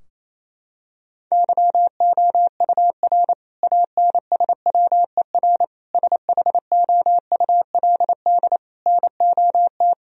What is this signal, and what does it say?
Morse code, heard as a keyed tone, about 28 wpm, 700 Hz. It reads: YOUR ANSWER SHOULD NOT
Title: Sentences from Top 1000 Words - Review Part 3 - 28wpm